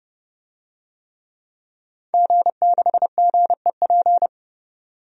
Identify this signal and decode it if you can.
Morse code, heard as a keyed tone, about 30 wpm, 700 Hz. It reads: G6GEP
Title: DXCC-100 Call Signs - 5 Characters - 30wpm